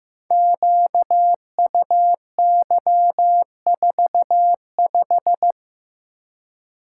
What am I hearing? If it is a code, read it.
Morse code, heard as a keyed tone, about 15 wpm, 700 Hz. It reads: QUY45
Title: Code Groups - 5 Letter:Numbers - 15wpm